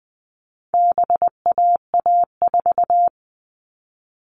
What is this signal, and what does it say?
Morse code, heard as a keyed tone, about 20 wpm, 700 Hz. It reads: BAA4
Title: All Cut Numbers - 4-Digit 20wpm